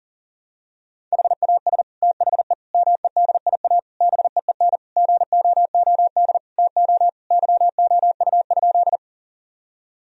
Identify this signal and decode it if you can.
Morse code, heard as a keyed tone, about 40 wpm, 700 Hz. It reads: HAS THE MEDIA BEEN GOOD TO YOU?
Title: Sentences from Top 1000 Words - Review Part 4 - 40wpm